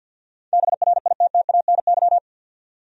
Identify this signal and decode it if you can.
Morse code, heard as a keyed tone, about 50 wpm, 700 Hz. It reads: BRITTANY
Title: USA Common Names - 50wpm